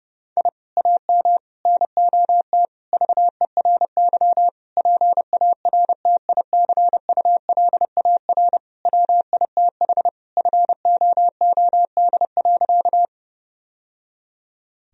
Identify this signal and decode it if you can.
Morse code, heard as a keyed tone, about 30 wpm, 700 Hz. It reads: I AM NOT VERY PARTICULAR WITH FOOD.
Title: Sentences from Top 1000 Words - Review Part 4 - 30wpm